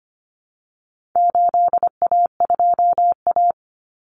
Morse code, heard as a keyed tone, about 25 wpm, 700 Hz. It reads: 8A2A
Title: Common Cut Numbers - 4-Digit 25wpm